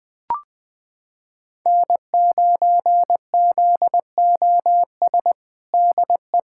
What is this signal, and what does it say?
Morse code, heard as a keyed tone, about 20 wpm, 700 Hz. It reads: N9ZOS DE
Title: USA Callsign DE Callsign - 5 Characters - 20wpm